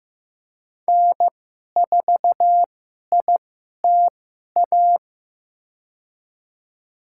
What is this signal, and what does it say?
Morse code, heard as a keyed tone, about 15 wpm, spaced as for 11 wpm, 700 Hz. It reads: N4ITA
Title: USA Call Signs - 5 Characters - 2x Character Spacing - 15wpm